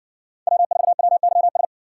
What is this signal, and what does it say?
Morse code, heard as a keyed tone, about 60 wpm, 700 Hz. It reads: W4WYS